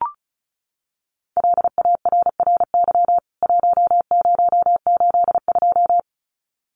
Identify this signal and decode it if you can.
Morse code, heard as a keyed tone, about 34 wpm, 700 Hz. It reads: LARRY 1082
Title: CWT Exchange 35wpm